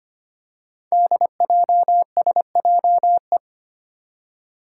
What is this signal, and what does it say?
Morse code, heard as a keyed tone, about 25 wpm, 700 Hz. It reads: DJSJE
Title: Five Challenging Letter-Number Combinations 25wpm